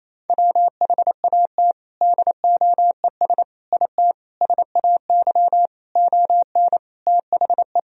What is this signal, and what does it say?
Morse code, heard as a keyed tone, about 28 wpm, 700 Hz. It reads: WHAT DOES IT SAY ON THE
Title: Sentences from Top 1000 Words - Review Part 2 - 28wpm